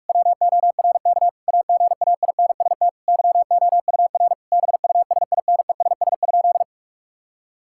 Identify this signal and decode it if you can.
Morse code, heard as a keyed tone, about 46 wpm, 690 Hz. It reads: WORK AGAINST YOUR BUSINESS?
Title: Sentences from Top 1000 Words - Review Part 5 - 45wpm